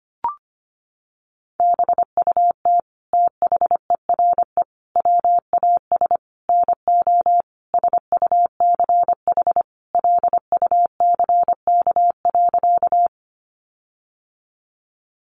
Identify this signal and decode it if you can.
Morse code, heard as a keyed tone, about 25 wpm, 700 Hz. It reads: BUT THERE WAS NO SUCH LUCK.
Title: Sentences from Top 1500 Words 25wpm